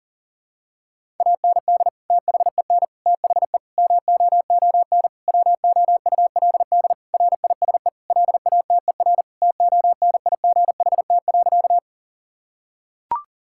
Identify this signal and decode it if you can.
Morse code, heard as a keyed tone, about 37 wpm, 700 Hz. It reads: AND THEN THE MOON WOULD RISE LATER TONIGHT.
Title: Sentences from Top 1000 Words - Review Part 6 - 40wpm